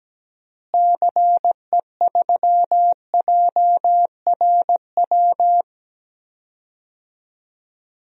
Morse code, heard as a keyed tone, about 17 wpm, 700 Hz. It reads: CE3JRW
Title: DXCC-100 Call Signs - 6 Characters - Repeated 2x - 17wpm